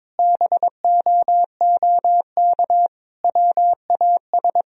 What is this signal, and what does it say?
Morse code, heard as a keyed tone, about 22 wpm, 700 Hz. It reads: BOOK WAS
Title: Sentences from Top 1000 Words - Review Part 3 - 22wpm